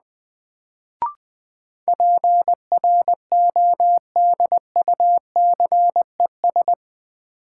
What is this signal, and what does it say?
Morse code, heard as a keyed tone, about 20 wpm, 700 Hz. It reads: PRODUCES
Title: Sentences from Top 1000 Words - Review Part 6 - New Word Drill - 20wpm